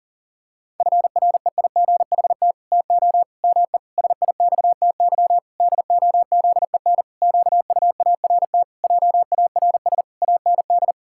Morse code, heard as a keyed tone, about 40 wpm, 700 Hz. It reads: FREIGHT TO ME SIXTY DOZEN QUART JARS AND
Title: Pangram Challenge v2 - 40wpm